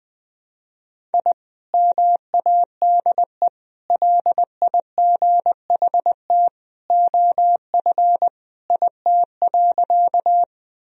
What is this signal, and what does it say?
Morse code, heard as a keyed tone, about 21 wpm, 700 Hz. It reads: I MADE LIGHT OF IT.
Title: Sentences from Top 1000 Words - Review Part 1 - 20wpm